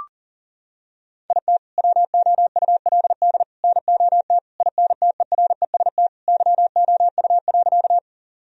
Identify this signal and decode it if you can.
Morse code, heard as a keyed tone, about 37 wpm, 700 Hz. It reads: IT WOULD NOT INTEREST YOU.
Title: Sentences from Top 1000 Words - Review Part 2 - 40wpm